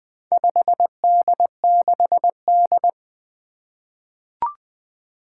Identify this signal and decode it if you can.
Morse code, heard as a keyed tone, about 20 wpm, 700 Hz. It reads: 5D6D
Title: Four Challenging Letter-Number Combinations 20wpm